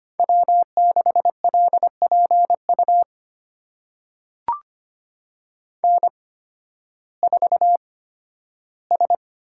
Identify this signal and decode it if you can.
Morse code, heard as a keyed tone, about 25 wpm, spaced as for 13 wpm, 700 Hz. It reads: W6LPU N4S
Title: USA Call Signs - 5 Characters - 8x Character Spacing - 25wpm